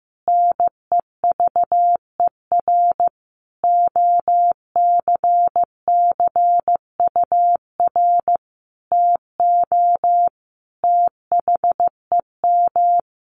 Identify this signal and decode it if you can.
Morse code, heard as a keyed tone, about 15 wpm, 700 Hz. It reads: NEVER OCCUR TO THEM
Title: Sentences from Top 900 Words 15wpm